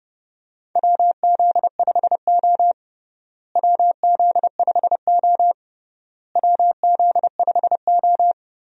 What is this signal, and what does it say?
Morse code, heard as a keyed tone, about 32 wpm, 700 Hz. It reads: WZ5O WZ5O WZ5O
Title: USA Call Signs - 4 Characters - Repeated 3x - 30wpm